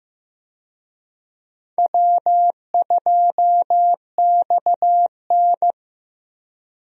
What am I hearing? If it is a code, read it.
Morse code, heard as a keyed tone, about 15 wpm, 700 Hz. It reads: W2XN
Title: USA Call Signs - 4 Characters - Repeated 2x - 15wpm